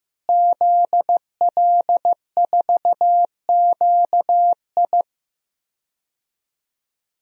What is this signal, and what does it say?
Morse code, heard as a keyed tone, about 15 wpm, 700 Hz. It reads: ZL4QI
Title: DXCC-100 Call Signs - 5 Characters - Repeated 3x - 15wpm